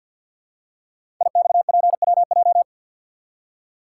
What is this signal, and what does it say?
Morse code, heard as a keyed tone, about 53 wpm, 700 Hz. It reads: IXPWJ
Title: Code Groups - 5 Letters - 50wpm